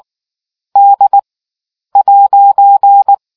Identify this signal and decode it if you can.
Morse code, heard as a keyed tone, about 19 wpm, spaced as for 9 wpm, 800 Hz. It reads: D'